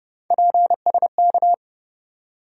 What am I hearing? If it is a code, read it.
Morse code, heard as a keyed tone, about 30 wpm, 700 Hz. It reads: PSK
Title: USA Call Sign Postfixes - 3 Letters 30wpm